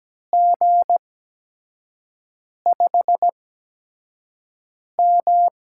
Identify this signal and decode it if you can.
Morse code, heard as a keyed tone, about 17 wpm, spaced as for 5 wpm, 700 Hz. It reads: G5M
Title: USA Call Signs - 6 Characters - 8x Character Spacing - 17wpm